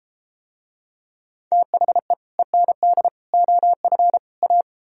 Morse code, heard as a keyed tone, about 33 wpm, 700 Hz. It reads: THE END OF A